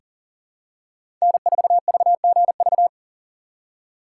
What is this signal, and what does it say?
Morse code, heard as a keyed tone, about 40 wpm, 700 Hz. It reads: N4VGV
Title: USA Call Signs - 5 Characters - 4x Character Spacing - 40wpm